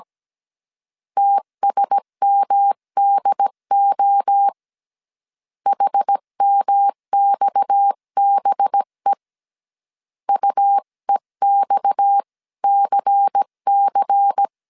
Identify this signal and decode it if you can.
Morse code, heard as a keyed tone, about 17 wpm, 800 Hz. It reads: TSMDO HMXBE UEXCC
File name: CW / DET / 1 6 1500C S C 16wpm 2733